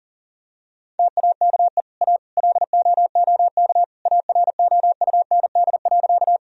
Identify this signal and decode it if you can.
Morse code, heard as a keyed tone, about 37 wpm, 700 Hz. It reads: TAKE A LOOK AROUND.
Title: Sentences from Top 1000 Words - Review Part 3 - 40wpm